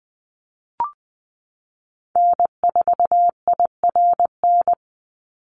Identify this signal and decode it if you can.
Morse code, heard as a keyed tone, about 20 wpm, 700 Hz. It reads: N4IRN
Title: Call Signs Repeated 3x - 20wpm